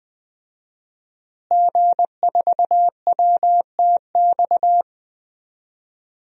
Morse code, heard as a keyed tone, about 20 wpm, 700 Hz. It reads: G4WTX
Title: DXCC-100 Call Signs - 5 Characters - Repeated 3x - 20wpm